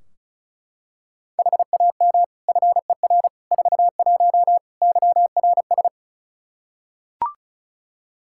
Morse code, heard as a keyed tone, about 34 wpm, 700 Hz. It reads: HAM FER 41 YRS
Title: Practice QSO - Exchange 3 - More Info - 35wpm